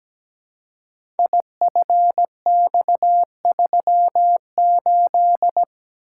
Morse code, heard as a keyed tone, about 17 wpm, 700 Hz. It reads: IFX38
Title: Code Groups - 5 Letter:Numbers - 17wpm